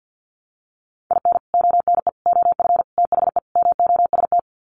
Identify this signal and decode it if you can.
Morse code, heard as a keyed tone, about 50 wpm, 700 Hz. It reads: IN ONE OF THE MOST
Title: Sets of 5 Words Encore 50wpm